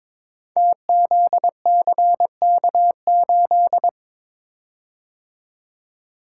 Morse code, heard as a keyed tone, about 22 wpm, 700 Hz. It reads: TZCK8
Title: Code Groups - 5 Letter:Numbers - 22wpm